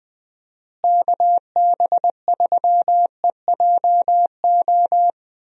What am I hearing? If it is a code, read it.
Morse code, heard as a keyed tone, about 20 wpm, 700 Hz. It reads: KB3EJO
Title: USA Call Signs - 6 Characters - Repeated 3x - 20wpm